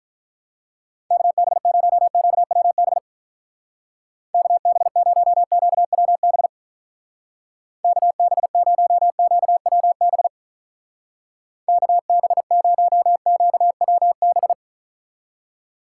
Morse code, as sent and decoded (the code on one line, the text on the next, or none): -.- -... ----- --.- .-- -... / -.- -... ----- --.- .-- -... / -.- -... ----- --.- .-- -... / -.- -... ----- --.- .-- -...
KB0QWB KB0QWB KB0QWB KB0QWB